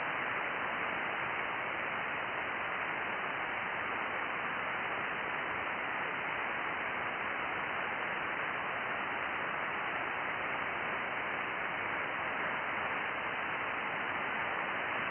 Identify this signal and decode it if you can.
FT8.
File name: IC-7100